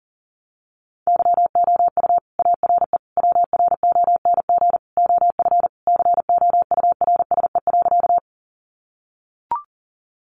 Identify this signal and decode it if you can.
Morse code, heard as a keyed tone, about 40 wpm, 700 Hz. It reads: YOU ARE WRONG OF COURSE.
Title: Sentences from Top 900 Words 40wpm